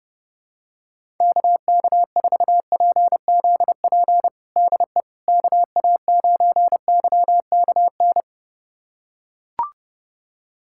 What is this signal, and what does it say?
Morse code, heard as a keyed tone, about 30 wpm, 700 Hz. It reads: KK4PZP DE KA9YKN
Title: USA Callsign DE Callsign - 6 Characters - 30wpm